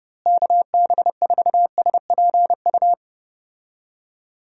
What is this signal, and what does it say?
Morse code, heard as a keyed tone, about 30 wpm, 700 Hz. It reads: KB4SPU
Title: USA Call Signs - 6 Characters - Repeated 3x - 30wpm